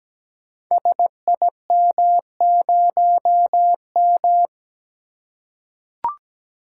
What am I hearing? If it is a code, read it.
Morse code, heard as a keyed tone, about 17 wpm, 700 Hz. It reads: SIM0M
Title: Code Groups - 5 Letter:Numbers - 17wpm